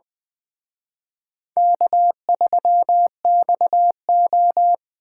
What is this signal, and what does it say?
Morse code, heard as a keyed tone, about 21 wpm, 700 Hz. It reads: K3XO
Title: Speed Racing - USA 1x2 Call Signs 17wpm